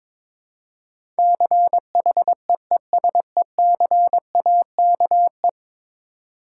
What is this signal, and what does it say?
Morse code, heard as a keyed tone, about 22 wpm, 700 Hz. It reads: CHEESECAKE
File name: Difficult Words with Hints - 22wpm